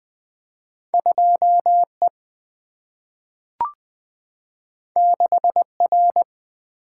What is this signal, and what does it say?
Morse code, heard as a keyed tone, about 20 wpm, 700 Hz. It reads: 2 E 6 R
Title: Single-Letter Single-Number 20wpm